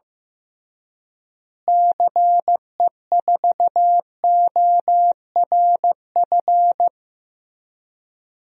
Morse code, heard as a keyed tone, about 15 wpm, 700 Hz. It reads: CE4ORF
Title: DXCC-100 Call Signs - 6 Characters - Repeated 3x - 15wpm